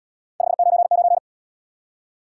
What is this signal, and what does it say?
Morse code, heard as a keyed tone, about 69 wpm, 700 Hz. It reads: 599